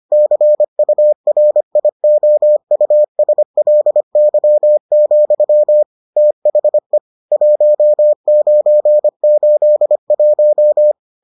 Morse code, heard as a keyed tone, about 25 wpm, 600 Hz. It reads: CURIOUSLY, THE 1981